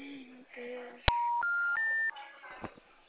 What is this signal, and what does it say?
Special information tone, with noise.
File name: out- -2021- - - .779